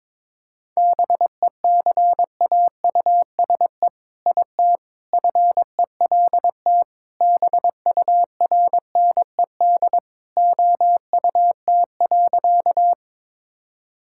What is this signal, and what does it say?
Morse code, heard as a keyed tone, about 22 wpm, 700 Hz. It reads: BECAUSE IT FELT BURNED OUT.
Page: morse-code-ninja-practice-22wpm